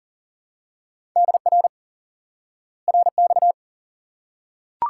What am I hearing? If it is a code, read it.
Morse code, heard as a keyed tone, about 40 wpm, 700 Hz. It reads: DR RX
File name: IQE Course 3 - Lesson 89 - YAGI - Two Elements - 40wpm